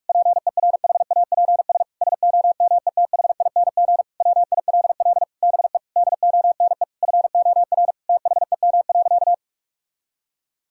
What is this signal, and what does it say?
Morse code, heard as a keyed tone, about 41 wpm, 700 Hz. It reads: PERHAPS SOMETHING WILL BE DONE FOR THEM.
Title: Sentences from Top 1600 Words 45wpm